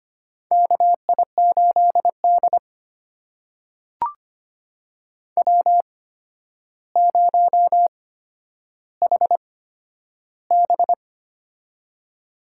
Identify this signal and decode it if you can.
Morse code, heard as a keyed tone, about 25 wpm, spaced as for 11 wpm, 700 Hz. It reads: KI8D W0HB
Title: USA Call Signs - 4 Characters - 8x Character Spacing - 25wpm